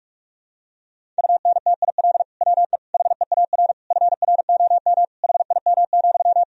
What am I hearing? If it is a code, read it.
Morse code, heard as a keyed tone, about 41 wpm, 700 Hz. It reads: UNTIL WE HEAR FROM HIM,
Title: Sentences from Top 1000 Words - Review Part 1 - 45wpm